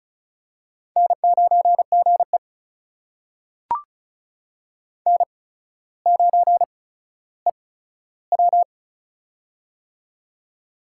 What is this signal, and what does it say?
Morse code, heard as a keyed tone, about 36 wpm, spaced as for 15 wpm, 700 Hz. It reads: N9GE N9EW
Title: USA Call Signs - 4 Characters - 8x Character Spacing - 35wpm